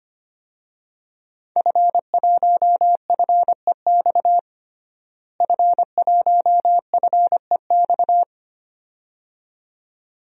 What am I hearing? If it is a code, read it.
Morse code, heard as a keyed tone, about 25 wpm, 700 Hz. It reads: F1FEX F1FEX
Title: DXCC-100 Call Signs - 5 Characters - Repeated 2x - 25wpm